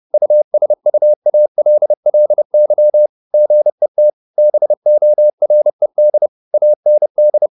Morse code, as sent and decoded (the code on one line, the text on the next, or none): ..- ... ..- .- .-.. .-.. -.-- / --. - / -... --- .-. -.. / .- -. -..
USUALLY GET BORED AND